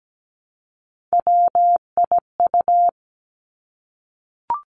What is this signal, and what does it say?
Morse code, heard as a keyed tone, about 17 wpm, 700 Hz. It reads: WIU
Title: Three Challenging Letter-Number Combinations 17wpm